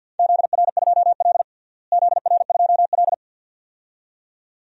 Morse code, heard as a keyed tone, about 50 wpm, 700 Hz. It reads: ZR2L ZR2L
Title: DXCC-100 Call Signs - 4 Characters - Repeated 2x - 50wpm